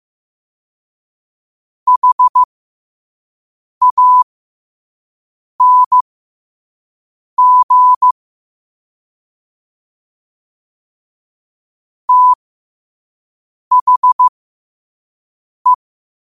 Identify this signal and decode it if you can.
Morse code, heard as a keyed tone, about 15 wpm, spaced as for 5 wpm, 1000 Hz. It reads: HANG THE
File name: qotdfiles / 5wpm